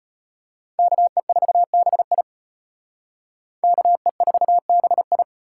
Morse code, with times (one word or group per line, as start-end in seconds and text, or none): 0.8-2.2 KE4BI
3.6-5.2 KE4BI